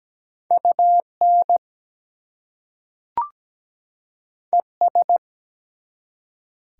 Morse code, heard as a keyed tone, about 17 wpm, 700 Hz. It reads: UN ES
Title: Prefixes and Suffixes 17wpm